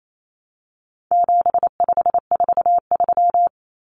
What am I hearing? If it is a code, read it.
Morse code, heard as a keyed tone, about 28 wpm, 700 Hz. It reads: 7543
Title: Common Cut Numbers - 4-Digit 28wpm